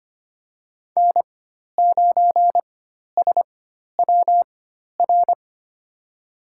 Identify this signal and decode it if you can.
Morse code, heard as a keyed tone, about 25 wpm, spaced as for 12 wpm, 700 Hz. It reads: N9SWR
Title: USA Call Signs - 4x Character Spacing - 25wpm